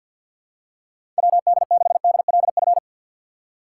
Morse code, heard as a keyed tone, about 52 wpm, 700 Hz. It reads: WD6DLF
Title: USA Call Signs - 6 Characters - 6x Character Spacing - 50wpm